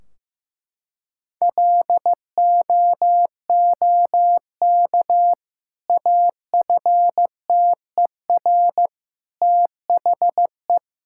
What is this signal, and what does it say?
Morse code, heard as a keyed tone, about 15 wpm, 700 Hz. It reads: LOOK AFTER THE
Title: Sentences from Top 1600 Words 15wpm